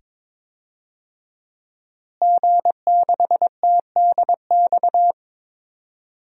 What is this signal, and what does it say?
Morse code, heard as a keyed tone, about 22 wpm, 700 Hz. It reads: G6TDX
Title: DXCC-100 Call Signs - 5 Characters - Repeated 2x - 22wpm